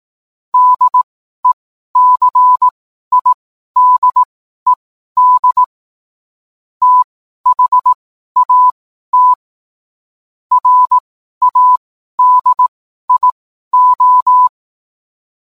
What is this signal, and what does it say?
Morse code, heard as a keyed tone, about 18 wpm, spaced as for 12 wpm, 1000 Hz. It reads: DECIDED THAT RADIO
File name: cwblog / cw mp3 directory / mem morse 12